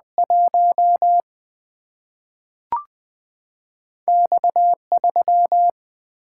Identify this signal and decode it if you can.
Morse code, heard as a keyed tone, about 20 wpm, 700 Hz. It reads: 1 X 3
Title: Single-Letter Single-Number 20wpm